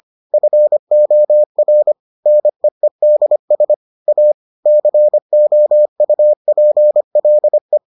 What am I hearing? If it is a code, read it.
Morse code, heard as a keyed tone, about 25 wpm, 600 Hz. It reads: FOR NEEDS A COUPLE